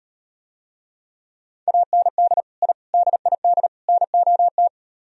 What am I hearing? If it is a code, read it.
Morse code, heard as a keyed tone, about 38 wpm, 700 Hz. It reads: AND I DID NOT